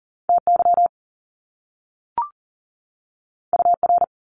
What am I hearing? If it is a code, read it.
Morse code, heard as a keyed tone, about 38 wpm, 700 Hz. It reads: TY UR
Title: Top Two Letter Combinations in Words 40wpm